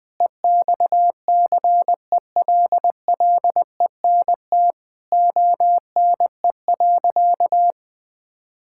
Morse code, heard as a keyed tone, about 20 wpm, 700 Hz. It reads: EXCELLENT ONE.
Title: Sentences from Top 1000 Words - Review Part 6 - 20wpm